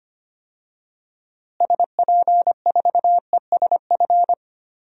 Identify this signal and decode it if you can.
Morse code, heard as a keyed tone, about 25 wpm, 700 Hz. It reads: SP4ESF
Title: DXCC-100 Call Signs - 6 Characters - Repeated 2x - 25wpm